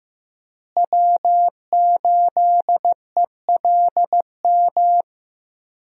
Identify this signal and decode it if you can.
Morse code, heard as a keyed tone, about 15 wpm, 700 Hz. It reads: W8ELM
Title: USA Call Signs - 5 Characters - Repeated 2x - 15wpm